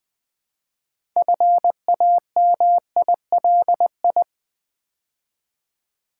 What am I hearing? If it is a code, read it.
Morse code, heard as a keyed tone, about 20 wpm, 700 Hz. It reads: FAMILI
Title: Build A Word - 2 Letters At A Time - 8-Letter Words - 20wpm